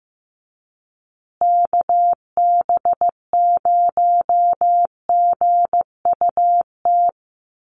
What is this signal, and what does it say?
Morse code, heard as a keyed tone, about 15 wpm, 700 Hz. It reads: KB0GUT
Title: USA Call Signs Repeated 2x 15wpm